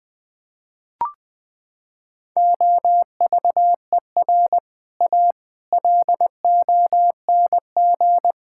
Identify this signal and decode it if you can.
Morse code, heard as a keyed tone, about 19 wpm, 700 Hz. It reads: OVER A LONG
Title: Sets of 6 Words - 20wpm